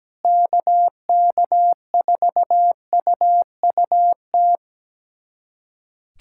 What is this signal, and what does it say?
Morse code, heard as a keyed tone, about 17 wpm, 700 Hz. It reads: KK4UUT
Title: USA Callsign DE Callsign - 6 Characters - 17wpm